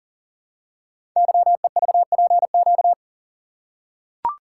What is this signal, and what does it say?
Morse code, heard as a keyed tone, about 38 wpm, 700 Hz. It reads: YEVPQ
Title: Code Groups - 5 Letters - 40wpm